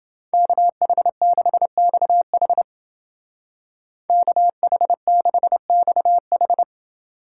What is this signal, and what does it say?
Morse code, heard as a keyed tone, about 29 wpm, 700 Hz. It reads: KH6XH KH6XH